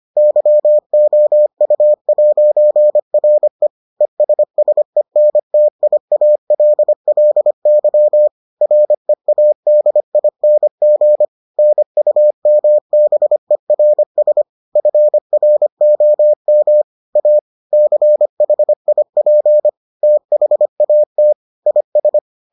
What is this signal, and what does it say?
Morse code, heard as a keyed tone, about 25 wpm, 600 Hz. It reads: YOU'RE ESSENTIALLY READING NUMBERS FROM A CHIP THAT IS